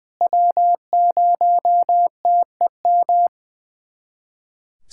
Morse code, heard as a keyed tone, about 20 wpm, 700 Hz. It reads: W0TEM
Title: SOTA QSO Practice 20wpm